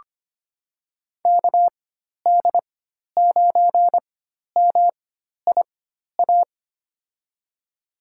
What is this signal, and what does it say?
Morse code, heard as a keyed tone, about 25 wpm, spaced as for 12 wpm, 700 Hz. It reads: KD9MIA
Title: USA Call Signs - 6 Characters - 4x Character Spacing - 25wpm